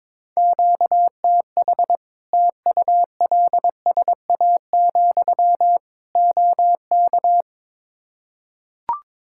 Morse code, heard as a keyed tone, about 22 wpm, 700 Hz. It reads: QTH TULSA, OK
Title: Practice QSO - Exchange 1 - Beginning - 22wpm